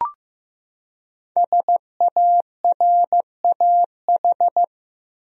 Morse code, heard as a keyed tone, about 15 wpm, 700 Hz. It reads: SARAH
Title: USA Common Names - 15wpm